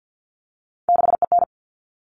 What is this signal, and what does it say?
Morse code, heard as a keyed tone, about 50 wpm, 700 Hz. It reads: 6EN
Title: All Cut Numbers - 3-Digit 50wpm